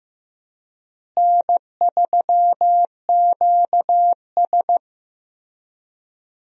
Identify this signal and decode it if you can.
Morse code, heard as a keyed tone, about 15 wpm, 700 Hz. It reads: N3QS
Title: USA Call Signs - 4 Characters - Repeated 3x - 15wpm